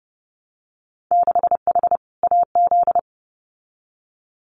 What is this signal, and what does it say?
Morse code, heard as a keyed tone, about 30 wpm, 700 Hz. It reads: 6H AZ
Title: Winter Field Day Exchanges Encore 30wpm